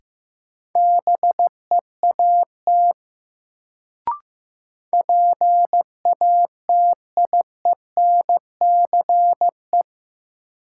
Morse code, heard as a keyed tone, about 15 wpm, 700 Hz. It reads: BEAT PATIENCE
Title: Sentences from Top 1300 Words - New Word Drill - 15wpm